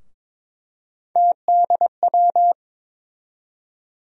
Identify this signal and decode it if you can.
Morse code, heard as a keyed tone, about 22 wpm, 700 Hz. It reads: TDW
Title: USA Call Sign Postfixes - 3 Letters 22wpm